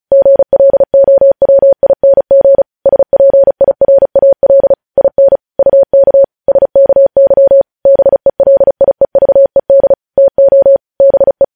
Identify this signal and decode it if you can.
Morse code, heard as a keyed tone, about 36 wpm, 550 Hz. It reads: GLOWING SPIRAL IN UK SKY BELIEVED TO BE
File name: podcast / News headlines-1x-35x35